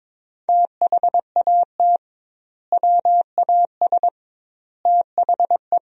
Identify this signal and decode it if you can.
Morse code, heard as a keyed tone, about 21 wpm, 700 Hz. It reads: THAT WAS THE
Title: 2x Word-Spacing - Sentences from Top 300 Words - 22wpm